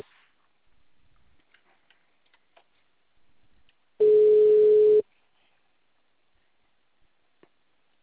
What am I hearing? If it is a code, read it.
Ringback tone (Europe, 425 Hz).
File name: out- -2051- - - .73852